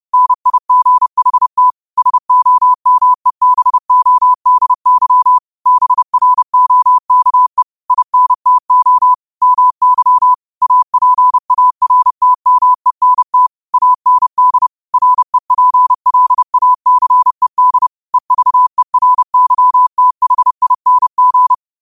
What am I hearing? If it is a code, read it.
Morse code, heard as a keyed tone, about 30 wpm, 1000 Hz. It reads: NIGHT SOMEBODY BROKE INTO MY APARTMENT AND REPLACED EVERYTHING
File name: qotdfiles / 30wpm